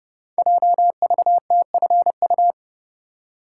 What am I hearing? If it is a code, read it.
Morse code, heard as a keyed tone, about 30 wpm, 700 Hz. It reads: JVTFU
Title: Code Groups - 5 Letter:Numbers - 30wpm